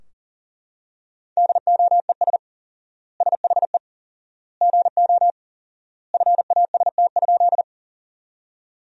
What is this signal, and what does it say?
Morse code, heard as a keyed tone, about 37 wpm, 700 Hz. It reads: DOES SHE GO FAST?
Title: 4x Word-Spacing - Sentences from Top 400 Words - 40wpm